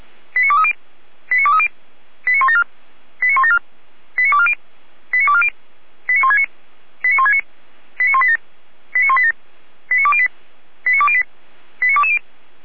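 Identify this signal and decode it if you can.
Five-tone selective call (ZVEI), heard as quick runs of five tones, tones of about 70 ms. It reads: 89230 89230 89174 89174 89240 89240 89169 89169 89187 89187 89208 89208 89220
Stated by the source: ZVEI five-tone selective call 89167 received on an SDR by the Feuerwehr Orlamünde fire brigade.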